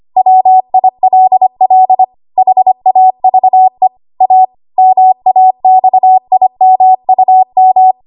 Morse code, heard as a keyed tone, about 25 wpm, 750 Hz. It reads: WILL HAVE A MAXIMUM
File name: Archive / 25 WPM / 25WPM